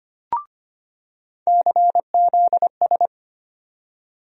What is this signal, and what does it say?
Morse code, heard as a keyed tone, about 25 wpm, 700 Hz. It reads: CZS
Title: USA Call Sign Postfixes - 3 Letters 25wpm